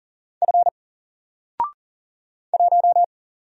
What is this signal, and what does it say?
Morse code, heard as a keyed tone, about 40 wpm, 700 Hz. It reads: F 1
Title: Single Letter-Number Alt 40wpm